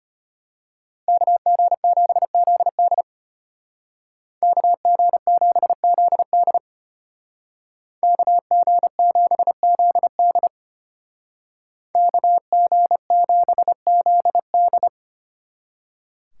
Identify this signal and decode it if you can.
Morse code, heard as a keyed tone, about 33 wpm, 700 Hz. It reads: KG7ZD KG7ZD KG7ZD KG7ZD